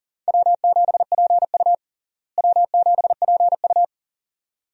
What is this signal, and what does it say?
Morse code, heard as a keyed tone, about 37 wpm, 700 Hz. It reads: W7PU W7PU
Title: USA Call Signs - 4 Characters - Repeated 2x - 40wpm